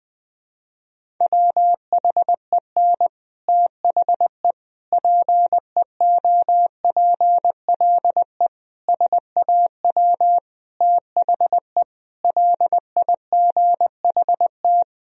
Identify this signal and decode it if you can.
Morse code, heard as a keyed tone, about 20 wpm, 700 Hz. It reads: WHEN THE PEOPLE SAW THE LIGHT
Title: Sentences from Top 1000 Words - Review Part 1 - 20wpm